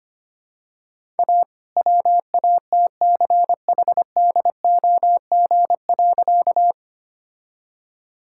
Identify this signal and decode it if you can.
Morse code, heard as a keyed tone, about 25 wpm, 700 Hz. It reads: A WATCHDOG.
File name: practice / 25wpm / All-Age Appropriate Jokes - 25wpm